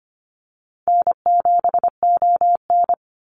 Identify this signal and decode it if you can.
Morse code, heard as a keyed tone, about 25 wpm, 700 Hz. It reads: N7ON
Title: Easy Call Signs - 25wpm